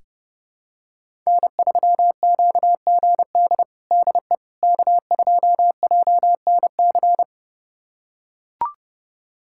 Morse code, heard as a keyed tone, about 30 wpm, 700 Hz. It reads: N3QGD DE K2JNC
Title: USA Callsign DE Callsign - 5 Characters - 30wpm